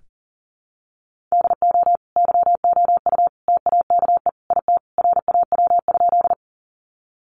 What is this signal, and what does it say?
Morse code, heard as a keyed tone, about 37 wpm, 700 Hz. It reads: DO YOU TAKE IT RAW?
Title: Sentences from Top 1500 Words 40wpm